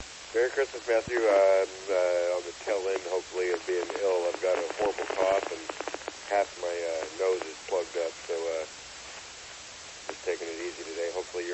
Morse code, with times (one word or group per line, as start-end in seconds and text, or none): none